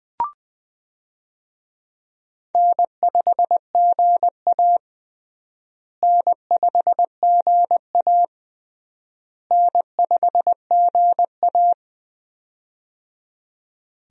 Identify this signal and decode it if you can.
Morse code, heard as a keyed tone, about 21 wpm, 700 Hz. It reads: N5GA N5GA N5GA
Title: USA Call Signs - 4 Characters - Repeated 3x - 20wpm